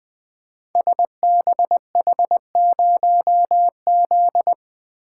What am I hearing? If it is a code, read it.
Morse code, heard as a keyed tone, about 21 wpm, 700 Hz. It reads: SBH0Z
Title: Code Groups - 5 Letter:Numbers - 20wpm